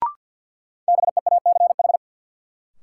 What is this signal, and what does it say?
Morse code, heard as a keyed tone, about 52 wpm, 700 Hz. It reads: BEACH
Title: Sentences from Top 1500 Words - New Word Drill - 50wpm